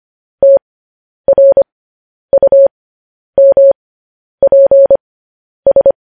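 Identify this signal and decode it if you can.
Morse code, heard as a keyed tone, about 25 wpm, spaced as for 10 wpm, 550 Hz. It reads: TRUMPS
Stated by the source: Morse code transcription of news headlines.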